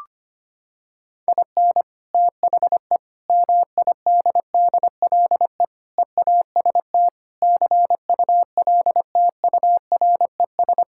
Morse code, heard as a keyed tone, about 25 wpm, 700 Hz. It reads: IN THE MIDDLE EAST CULTURES
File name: practice / 25wpm / Sentences with 4-Word Phrases - 25wpm